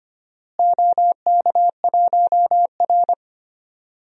Morse code, heard as a keyed tone, about 25 wpm, 700 Hz. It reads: OK1R
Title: DXCC-100 Call Signs - 4 Characters - Repeated 2x - 25wpm